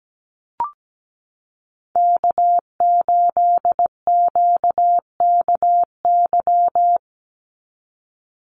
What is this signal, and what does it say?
Morse code, heard as a keyed tone, about 17 wpm, 700 Hz. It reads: K8QKY
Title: Call Signs Repeated 2x - 17wpm